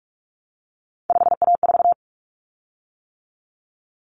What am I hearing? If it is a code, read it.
Morse code, heard as a keyed tone, about 44 wpm, 700 Hz. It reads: 5A4
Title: Common Cut Numbers - 3-Digit 45wpm